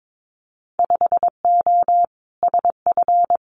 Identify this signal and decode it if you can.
Morse code, heard as a keyed tone, about 22 wpm, 700 Hz. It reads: 5O SF
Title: Winter Field Day Exchanges Encore 22wpm